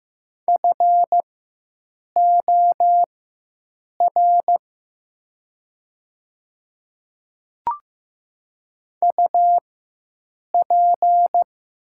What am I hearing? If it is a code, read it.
Morse code, heard as a keyed tone, about 15 wpm, spaced as for 7 wpm, 700 Hz. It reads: FOR UP
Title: 4x Character Spacing - Top 100 Words - 15wpm